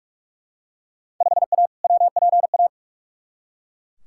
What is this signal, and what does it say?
Morse code, heard as a keyed tone, about 41 wpm, 700 Hz. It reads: 5A WPA
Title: Field Day Exchanges Encore - 45wpm